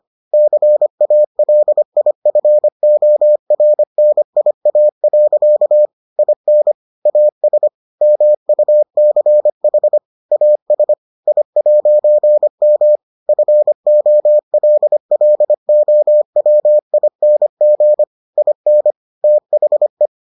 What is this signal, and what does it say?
Morse code, heard as a keyed tone, about 25 wpm, 600 Hz. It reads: CALIFORNIA. IN AS MUCH AS I'M FOLLOWING IN THE